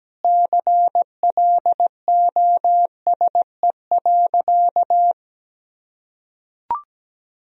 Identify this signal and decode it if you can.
Morse code, heard as a keyed tone, about 17 wpm, 700 Hz. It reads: CLOSE.
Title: Sentences from Top 1000 Words - Review Part 1 - 17wpm